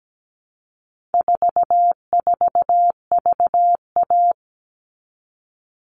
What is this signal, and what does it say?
Morse code, heard as a keyed tone, about 17 wpm, 700 Hz. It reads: 44VA
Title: All Cut Numbers - 4-Digit 17wpm